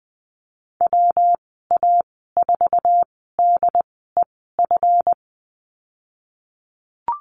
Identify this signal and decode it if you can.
Morse code, heard as a keyed tone, about 19 wpm, spaced as for 14 wpm, 700 Hz. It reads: WA4DEF
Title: 186 Lesson 30g - Letter b - Call Signs 20wpm